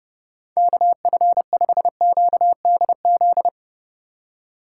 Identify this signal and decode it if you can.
Morse code, heard as a keyed tone, about 30 wpm, 700 Hz. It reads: KF5QDZ
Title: USA Call Signs - 6 Characters - 3x Character Spacing - 30wpm